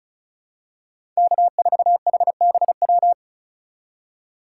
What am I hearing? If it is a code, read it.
Morse code, heard as a keyed tone, about 34 wpm, 700 Hz. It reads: K4HBW
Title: USA Call Signs - 5 Characters - 4x Character Spacing - 35wpm